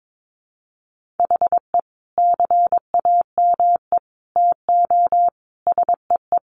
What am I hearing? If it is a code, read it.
Morse code, heard as a keyed tone, about 22 wpm, 700 Hz. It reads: HE CAME TO SEE
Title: Sentences from Top 1300 Words 22wpm